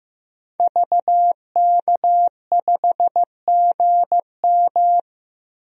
Morse code, heard as a keyed tone, about 15 wpm, 700 Hz. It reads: VK5GM
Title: DXCC-100 Call Signs - 5 Characters - 15wpm